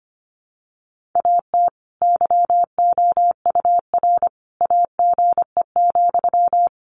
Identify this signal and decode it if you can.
Morse code, heard as a keyed tone, about 25 wpm, 700 Hz. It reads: AT YOUR AGE,
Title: Sentences from Top 1500 Words 25wpm